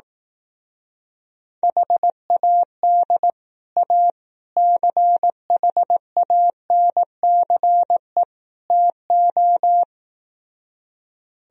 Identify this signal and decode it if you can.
Morse code, heard as a keyed tone, about 18 wpm, 700 Hz. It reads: HAD A CHANCE TO
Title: Speed-Racing - Sets of 4 Words 15wpm